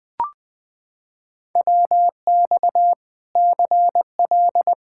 Morse code, heard as a keyed tone, about 20 wpm, 700 Hz. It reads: WX CL
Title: Practice QSO - Exchange 2 - Info - 20wpm